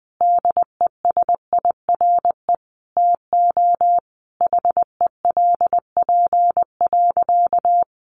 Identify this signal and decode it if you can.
Morse code, heard as a keyed tone, about 20 wpm, 700 Hz. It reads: DESIRE TO HELP.
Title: Sentences from Top 1100 Words 20wpm